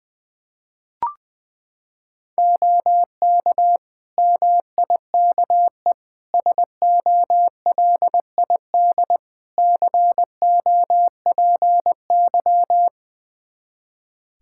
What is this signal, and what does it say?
Morse code, heard as a keyed tone, about 21 wpm, 700 Hz. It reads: OK MIKE SOLID COPY
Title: Practice QSO - Exchange 3 - More Info - 20wpm